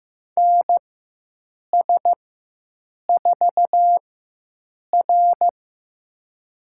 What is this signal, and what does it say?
Morse code, heard as a keyed tone, about 15 wpm, spaced as for 7 wpm, 700 Hz. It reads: NS4R
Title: USA Call Signs - 4 Characters - 4x Character Spacing - 15wpm